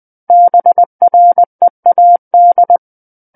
Morse code, heard as a keyed tone, about 20 wpm, 700 Hz. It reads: BREAD
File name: 20 WPM